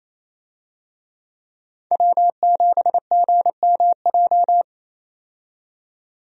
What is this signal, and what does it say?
Morse code, heard as a keyed tone, about 28 wpm, 700 Hz. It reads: W7GMJ
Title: USA Call Signs - 5 Characters - 28wpm